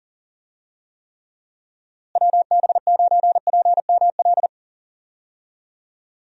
Morse code, heard as a keyed tone, about 39 wpm, 700 Hz. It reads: WB9PML
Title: USA Call Signs - 6 Characters - Repeated 2x - 40wpm